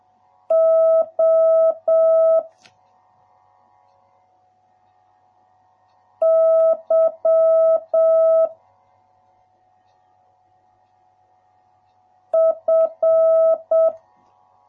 Morse code, heard as a keyed tone, about 7 wpm, 650 Hz. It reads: O Y F